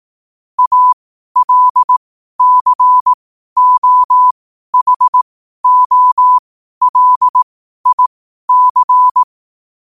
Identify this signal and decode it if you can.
Morse code, heard as a keyed tone, about 18 wpm, spaced as for 12 wpm, 1000 Hz. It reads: ALCOHOLIC
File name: qotdfiles / 12wpm